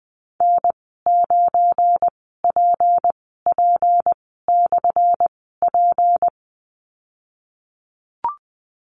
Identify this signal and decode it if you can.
Morse code, heard as a keyed tone, about 19 wpm, spaced as for 14 wpm, 700 Hz. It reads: N9PP/P
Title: 179 Lesson 29g - Slash - Call Signs 20wpm